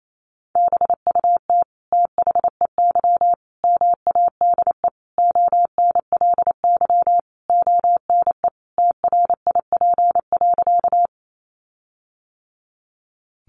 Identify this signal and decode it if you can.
Morse code, heard as a keyed tone, about 28 wpm, 700 Hz. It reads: BUT THEY MADE ONLY ONE TRIP.
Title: Sentences from Top 1300 Words 28wpm